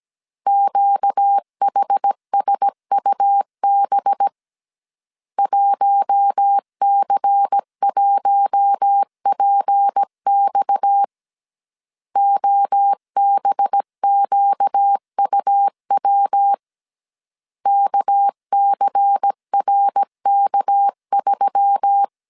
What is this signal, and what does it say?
Morse code, heard as a keyed tone, about 17 wpm, 800 Hz. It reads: QHSUB 1C1PX OBQUW KCRK3
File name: CW / DET / 1 6 1500C S A 16wpm 2635